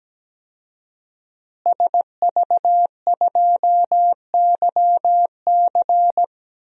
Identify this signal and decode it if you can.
Morse code, heard as a keyed tone, about 17 wpm, 700 Hz. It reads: SV2YC
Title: DXCC-100 Call Signs - 5 Characters - Repeated 3x - 17wpm